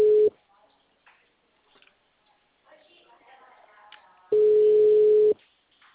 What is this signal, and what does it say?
Ringback tone (Europe, 425 Hz), with speech.
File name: out- -2003- - - .50582